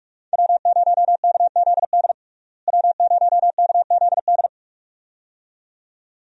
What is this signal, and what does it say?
Morse code, heard as a keyed tone, about 46 wpm, 700 Hz. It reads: W0KZD W0KZD
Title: USA Call Signs - 5 Characters - Repeated 2x - 45wpm